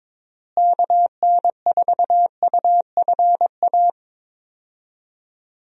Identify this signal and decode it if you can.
Morse code, heard as a keyed tone, about 22 wpm, 700 Hz. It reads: KN4UFA
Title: USA Call Signs - 6 Characters - Repeated 3x - 22wpm